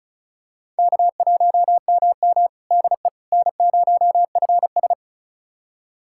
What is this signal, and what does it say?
Morse code, heard as a keyed tone, about 36 wpm, 700 Hz. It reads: K1MM DE N0FS
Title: USA Callsign DE Callsign - 4 Characters - 35wpm